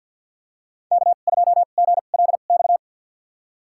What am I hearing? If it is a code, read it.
Morse code, heard as a keyed tone, about 52 wpm, spaced as for 36 wpm, 700 Hz. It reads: K2GLX
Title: USA Call Signs - 5 Characters - 2x Character Spacing - 50wpm